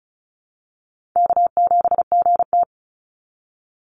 Morse code, heard as a keyed tone, about 36 wpm, 700 Hz. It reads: K7GT
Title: NA SOTA Chaser Callsigns 35wpm